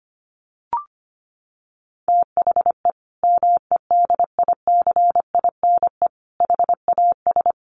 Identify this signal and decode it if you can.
Morse code, heard as a keyed tone, about 25 wpm, 700 Hz. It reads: THE MEDICINE HAS
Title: Sentences from Top 900 Words 25wpm